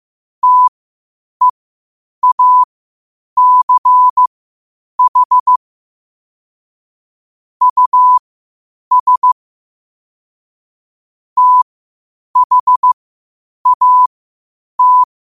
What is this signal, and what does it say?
Morse code, heard as a keyed tone, about 15 wpm, spaced as for 8 wpm, 1000 Hz. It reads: TEACH US THAT
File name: qotdfiles / 8wpm